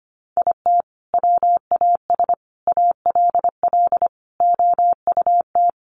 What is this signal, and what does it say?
Morse code, heard as a keyed tone, about 25 wpm, 700 Hz. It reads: IT WAS ALL OUT
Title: Sentences from Top 1200 Words 25wpm